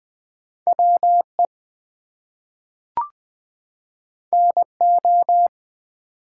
Morse code, heard as a keyed tone, about 20 wpm, 700 Hz. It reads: WE NO
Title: Two Letter Words - 20wpm